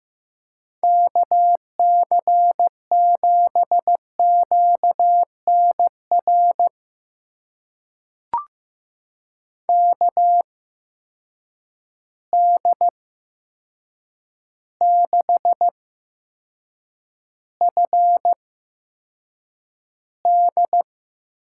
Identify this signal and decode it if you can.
Morse code, heard as a keyed tone, about 15 wpm, spaced as for 7 wpm, 700 Hz. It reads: KC7QNR KD6FD
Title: USA Call Signs - 6 Characters - 8x Character Spacing - 15wpm